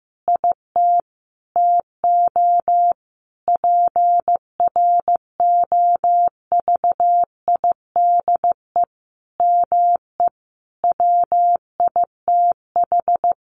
Morse code, heard as a keyed tone, about 15 wpm, 700 Hz. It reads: IT TO PROVIDE ME WITH
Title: Sentences from Top 1500 Words 15wpm